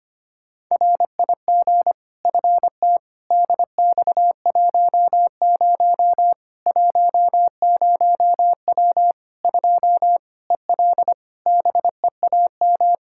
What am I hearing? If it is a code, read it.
Morse code, heard as a keyed tone, about 25 wpm, 700 Hz. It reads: RIG FT DX10 10W 2 EL BEAM
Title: Practice QSO - Exchange 2 - Info - 25wpm